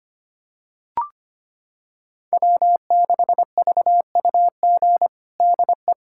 Morse code, heard as a keyed tone, about 25 wpm, 700 Hz. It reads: W6VUG DE
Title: USA Callsign DE Callsign - 5 Characters - 25wpm